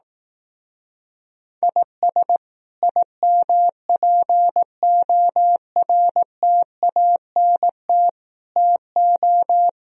Morse code, heard as a keyed tone, about 18 wpm, 700 Hz. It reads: IS IMPORTANT TO
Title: Speed-Racing - Sets of 5 Words 15wpm